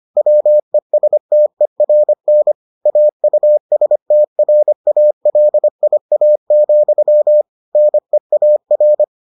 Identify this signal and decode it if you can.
Morse code, heard as a keyed tone, about 25 wpm, 600 Hz. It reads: WESTERN AUSTRALIA, NEAR